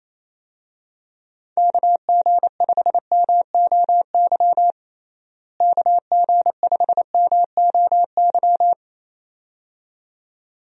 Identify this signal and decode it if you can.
Morse code, heard as a keyed tone, about 27 wpm, 700 Hz. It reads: KG5MOY KG5MOY
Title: USA Call Signs - 6 Characters - Repeated 2x - 28wpm